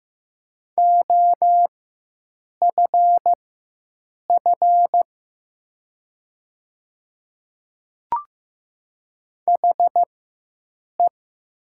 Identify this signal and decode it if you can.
Morse code, heard as a keyed tone, about 15 wpm, spaced as for 7 wpm, 700 Hz. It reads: OFF HE
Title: 4x Character Spacing - Top 200 Words - 15wpm